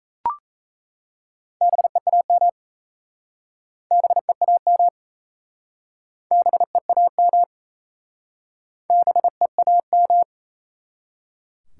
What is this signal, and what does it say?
Morse code, heard as a keyed tone, about 37 wpm, 700 Hz. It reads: BEAM BEAM BEAM BEAM